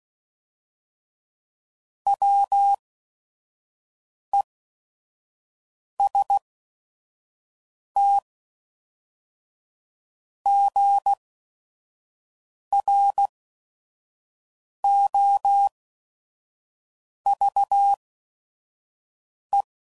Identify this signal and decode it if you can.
Morse code, heard as a keyed tone, about 16 wpm, spaced as for 5 wpm, 800 Hz. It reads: WESTGROVE
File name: CW Code Groups @5WPM / fromSM's / Cities